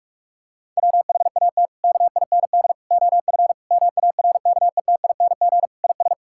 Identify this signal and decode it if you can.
Morse code, heard as a keyed tone, about 46 wpm, 700 Hz. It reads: WHAT KIND OF MARKETING IS
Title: Sentences from Top 1000 Words - Review Part 5 - 45wpm